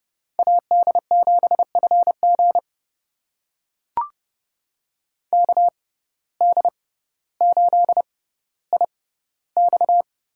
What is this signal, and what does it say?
Morse code, heard as a keyed tone, about 30 wpm, spaced as for 15 wpm, 700 Hz. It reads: AD7FG KD8IX
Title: USA Call Signs - 5 Characters - 6x Character Spacing - 30wpm